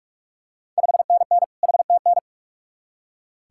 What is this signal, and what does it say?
Morse code, heard as a keyed tone, about 46 wpm, 700 Hz. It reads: 5NN HTN
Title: Ontario QSO Party Exchange 45wpm